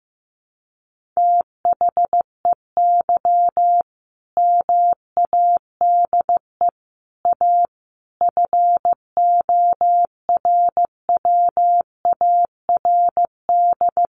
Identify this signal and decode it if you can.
Morse code, heard as a keyed tone, about 15 wpm, 700 Hz. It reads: THEY MADE A FORWARD
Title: Sentences from Top 1300 Words 15wpm